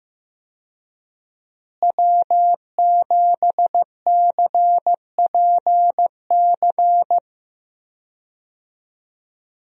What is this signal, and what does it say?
Morse code, heard as a keyed tone, about 15 wpm, 700 Hz. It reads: W7CPC
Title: USA Call Signs - 5 Characters - Repeated 2x - 15wpm